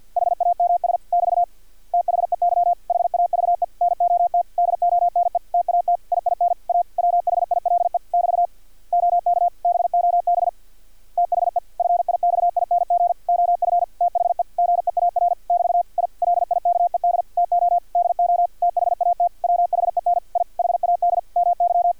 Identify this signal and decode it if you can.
Morse code, heard as a keyed tone, about 50 wpm, 700 Hz, with radio noise. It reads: HAMS <BT> THEY HAVE NOT DONE TAT IIN A WHILE <BT> OK BOB THE FIXING OF THE GEAR <BT> I LIKED TO DO THAT WHEN I HAD MY
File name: archived / CRSnet